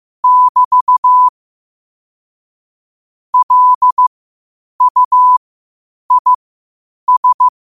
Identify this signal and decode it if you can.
Morse code, heard as a keyed tone, about 15 wpm, spaced as for 8 wpm, 1000 Hz. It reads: <BT> LUIS
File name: 8wpm